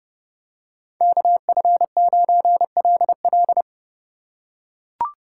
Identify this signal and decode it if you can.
Morse code, heard as a keyed tone, about 30 wpm, 700 Hz. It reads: KF9LL